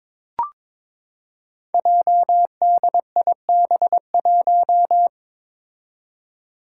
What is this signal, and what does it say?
Morse code, heard as a keyed tone, about 22 wpm, 700 Hz. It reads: JDIB1
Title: Five Challenging Letter-Number Combinations 22wpm